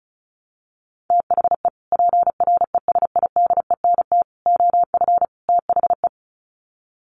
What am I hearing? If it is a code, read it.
Morse code, heard as a keyed tone, about 34 wpm, 700 Hz. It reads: THE PRESIDENT OF THE
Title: Sets of 4 Words 35wpm